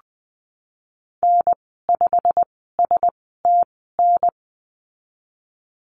Morse code, heard as a keyed tone, about 20 wpm, spaced as for 15 wpm, 700 Hz. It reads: N5STN
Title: 167 Lesson 27g - Number 7 - Call Signs 20wpm